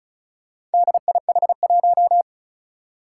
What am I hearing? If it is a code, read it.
Morse code, heard as a keyed tone, about 36 wpm, 700 Hz. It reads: DIH1
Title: Four Challenging Letter-Number Combinations 35wpm